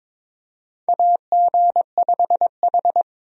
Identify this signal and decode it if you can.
Morse code, heard as a keyed tone, about 22 wpm, 700 Hz. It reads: AG5H
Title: USA Call Signs - 4 Characters - 22wpm